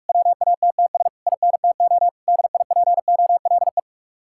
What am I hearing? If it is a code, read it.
Morse code, heard as a keyed tone, about 57 wpm, 700 Hz. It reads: WATTS INTO DIPOLE